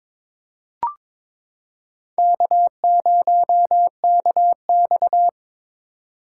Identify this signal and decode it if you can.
Morse code, heard as a keyed tone, about 22 wpm, 700 Hz. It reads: K0KX
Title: K1USN SST QSO Practice - 22wpm